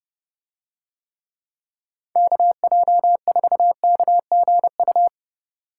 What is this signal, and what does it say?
Morse code, heard as a keyed tone, about 30 wpm, 700 Hz. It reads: KJ4KGU